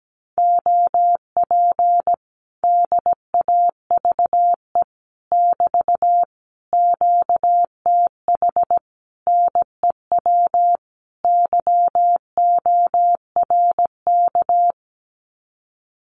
Morse code, heard as a keyed tone, about 17 wpm, 700 Hz. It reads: OP DAVE <BT> QTH NEW YORK
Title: Common QSO Phrases Encore 17wpm